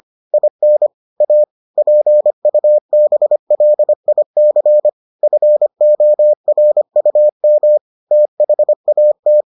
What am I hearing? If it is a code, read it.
Morse code, heard as a keyed tone, about 25 wpm, 600 Hz. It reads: IN A PUBLIC FORUM THAT